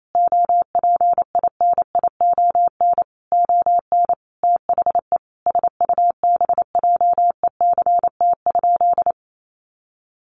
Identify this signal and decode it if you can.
Morse code, heard as a keyed tone, about 28 wpm, 700 Hz. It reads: OPINION ON THE SUBJECT?
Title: Sentences from Top 1200 Words 28wpm